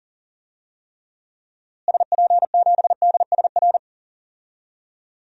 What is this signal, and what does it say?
Morse code, heard as a keyed tone, about 37 wpm, 700 Hz. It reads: SP7DSR
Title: DXCC-100 Call Signs - 6 Characters - Repeated 3x - 40wpm